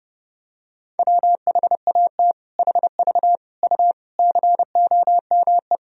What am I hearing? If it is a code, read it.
Morse code, heard as a keyed tone, about 30 wpm, 700 Hz. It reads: WHAT HV U COME
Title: Sentences from Top 100 Words and Abbreviations 30wpm